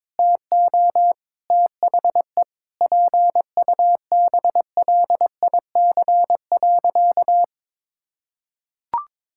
Morse code, heard as a keyed tone, about 22 wpm, 700 Hz. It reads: TO THE PUBLIC.
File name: Sentences with 7-Word Phrases - 22wpm